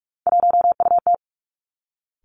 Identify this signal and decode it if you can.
Morse code, heard as a keyed tone, about 46 wpm, 700 Hz. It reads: 1UT
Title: Three Challenging Letter-Number Combinations 45wpm